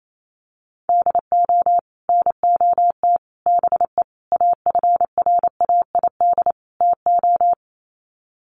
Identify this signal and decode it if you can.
Morse code, heard as a keyed tone, about 28 wpm, 700 Hz. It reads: DO NOT BE AFRAID TO
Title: Sets of 5 Words Encore2 28wpm